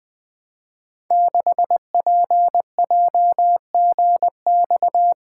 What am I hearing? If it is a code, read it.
Morse code, heard as a keyed tone, about 20 wpm, 700 Hz. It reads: 6PJGX
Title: Code Groups - 5 Letter:Numbers - 20wpm